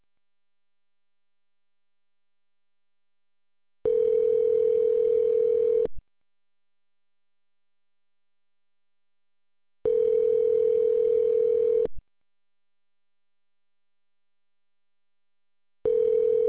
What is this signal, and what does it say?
Ringback tone (North America), 440 + 480 Hz.